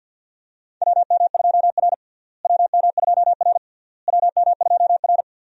Morse code, heard as a keyed tone, about 50 wpm, 700 Hz. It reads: WM2R WM2R WM2R
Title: USA Call Signs - 4 Characters - Repeated 3x - 50wpm